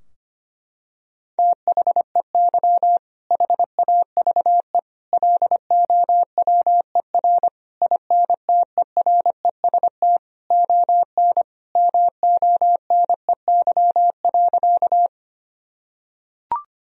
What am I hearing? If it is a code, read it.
Morse code, heard as a keyed tone, about 25 wpm, 700 Hz. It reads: THEY HAVE LOWER INTEREST ON MONEY.
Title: Sentences from Top 1000 Words - Review Part 4 - 25wpm